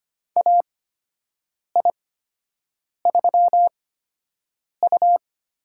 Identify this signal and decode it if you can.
Morse code, heard as a keyed tone, about 25 wpm, spaced as for 7 wpm, 700 Hz. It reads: AI3U
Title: USA Call Signs - 8x Character Spacing - 25wpm